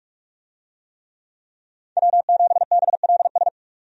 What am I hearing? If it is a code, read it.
Morse code, heard as a keyed tone, about 46 wpm, 700 Hz. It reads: W7BLS
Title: USA Call Signs - 5 Characters - 45wpm